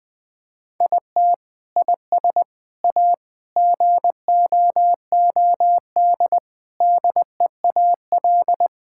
Morse code, heard as a keyed tone, about 20 wpm, 700 Hz. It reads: IT IS A GOOD DEAL
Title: Sentences with 7-Word Phrases - 20wpm